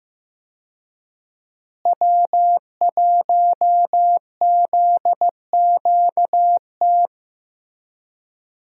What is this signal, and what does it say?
Morse code, heard as a keyed tone, about 15 wpm, 700 Hz. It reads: W1ZQT
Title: USA Call Signs - 5 Characters - Repeated 2x - 15wpm